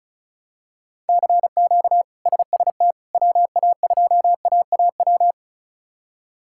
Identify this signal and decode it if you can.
Morse code, heard as a keyed tone, about 36 wpm, 700 Hz. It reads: CQ SST WA2AAW
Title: K1USN SST QSO Practice - 35wpm